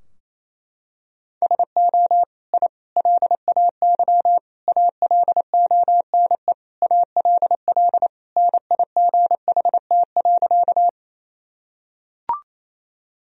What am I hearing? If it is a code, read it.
Morse code, heard as a keyed tone, about 28 wpm, 700 Hz. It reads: SO I LAY ALONE ALL NIGHT.
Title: Sentences from Top 1000 Words - Review Part 2 - 28wpm